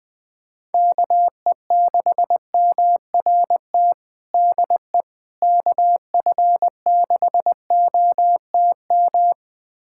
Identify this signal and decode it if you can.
Morse code, heard as a keyed tone, about 21 wpm, 700 Hz. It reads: KE6MRT DE KF6OTM
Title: USA Callsign DE Callsign - 6 Characters - 20wpm